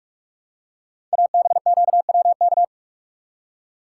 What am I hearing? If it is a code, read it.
Morse code, heard as a keyed tone, about 44 wpm, 700 Hz. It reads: ABQWK
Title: Code Groups - 5 Letters - 45wpm